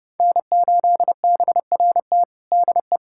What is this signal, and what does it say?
Morse code, heard as a keyed tone, about 30 wpm, 700 Hz. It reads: N8BRT DE
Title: USA Callsign DE Callsign - 5 Characters - 30wpm